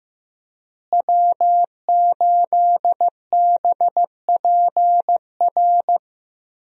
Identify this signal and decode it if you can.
Morse code, heard as a keyed tone, about 15 wpm, 700 Hz. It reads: W8BPR
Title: USA Call Signs - 5 Characters - Repeated 3x - 15wpm